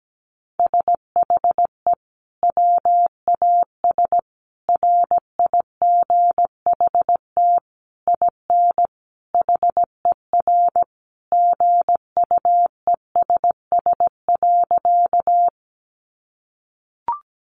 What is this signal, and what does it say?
Morse code, heard as a keyed tone, about 17 wpm, 700 Hz. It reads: SHE WAS RIGHT IN HER GUESS.
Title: Sentences from Top 1500 Words 17wpm